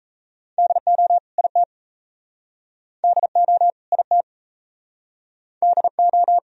Morse code, heard as a keyed tone, about 40 wpm, 700 Hz. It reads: DO IT DO IT DO